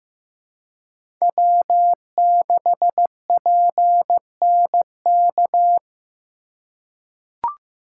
Morse code, heard as a keyed tone, about 16 wpm, 700 Hz. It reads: W6PNK
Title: USA Call Signs - 5 Characters - 3x Character Spacing - 15wpm